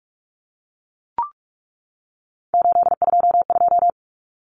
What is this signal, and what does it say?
Morse code, heard as a keyed tone, about 44 wpm, 700 Hz. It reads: 822
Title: Numbers - 3-Digit 45wpm